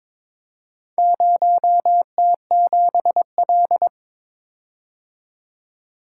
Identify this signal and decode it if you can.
Morse code, heard as a keyed tone, about 23 wpm, 700 Hz. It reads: 0T7L
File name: practice / 22wpm / Four Letter-Number Combinations - 22wpm